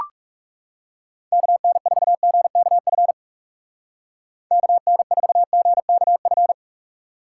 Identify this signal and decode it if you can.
Morse code, heard as a keyed tone, about 38 wpm, 700 Hz. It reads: KN4GKF KN4GKF